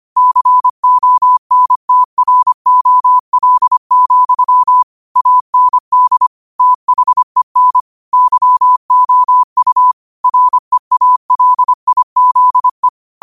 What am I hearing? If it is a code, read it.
Morse code, heard as a keyed tone, about 25 wpm, 1000 Hz. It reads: CONTROL, AND THEN YOU REALIZE
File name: qotdfiles / 25wpm